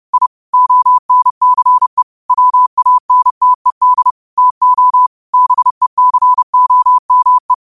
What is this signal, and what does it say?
Morse code, heard as a keyed tone, about 30 wpm, 1000 Hz. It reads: I ONCE WANTED TO BECOME
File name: qotdfiles / 30wpm